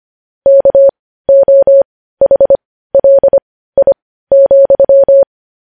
Morse code, heard as a keyed tone, about 25 wpm, spaced as for 15 wpm, 550 Hz. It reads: KOHLI,
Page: news-headlines-in-morse-code-at-15-wpm